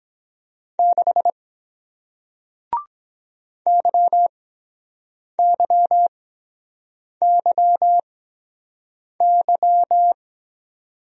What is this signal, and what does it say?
Morse code, heard as a keyed tone, about 23 wpm, spaced as for 7 wpm, 700 Hz. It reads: 6 YYYY